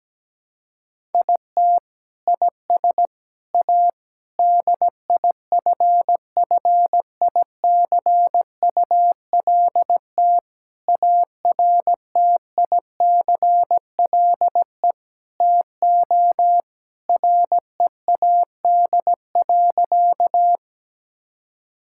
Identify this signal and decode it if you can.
Morse code, heard as a keyed tone, about 17 wpm, 700 Hz. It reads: IT IS A DIFFICULT ARTICLE TO READ.
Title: Sentences from Top 1000 Words - Review Part 5 - 17wpm